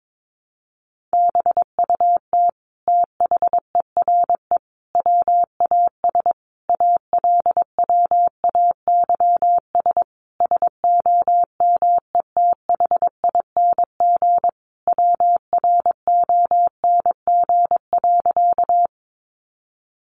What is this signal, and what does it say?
Morse code, heard as a keyed tone, about 22 wpm, 700 Hz. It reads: BUT THERE WAS ALWAYS SOMETHING WRONG.
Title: Sentences from Top 900 Words 22wpm